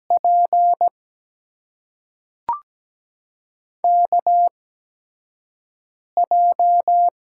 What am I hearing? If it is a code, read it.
Morse code, heard as a keyed tone, about 17 wpm, spaced as for 4 wpm, 700 Hz. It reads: PKJ
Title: USA Call Signs - 6 Characters - 8x Character Spacing - 17wpm